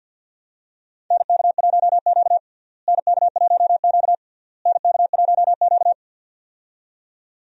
Morse code, heard as a keyed tone, about 48 wpm, 700 Hz. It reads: NK1Q NK1Q NK1Q
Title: USA Call Signs - 4 Characters - Repeated 3x - 50wpm